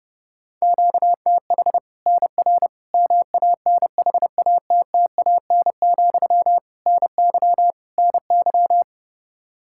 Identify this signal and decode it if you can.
Morse code, heard as a keyed tone, about 30 wpm, 700 Hz. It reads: QTH NR MANHATTAN, NY NY
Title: Practice QSO - Exchange 1 - Beginning - 30wpm